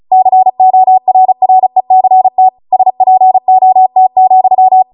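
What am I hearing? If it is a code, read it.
Morse code, heard as a keyed tone, about 36 wpm, 750 Hz. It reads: CORRECT SPOT,
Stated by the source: Amateur radio Morse code practice archive transmission at 35 wpm.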